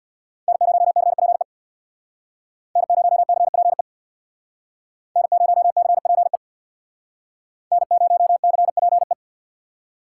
Morse code, heard as a keyed tone, about 60 wpm, 700 Hz. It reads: N0CPE N0CPE N0CPE N0CPE
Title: Speed Racing - USA 1x3 Call Signs 50wpm